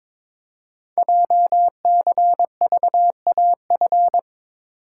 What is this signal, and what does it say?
Morse code, heard as a keyed tone, about 22 wpm, 700 Hz. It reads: JCVAF
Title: Code Groups - 5 Letters - 22wpm